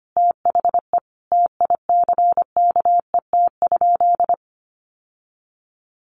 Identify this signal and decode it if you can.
Morse code, heard as a keyed tone, about 25 wpm, 700 Hz. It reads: THE TICKET?
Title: Sentences from Top 1400 Words 25wpm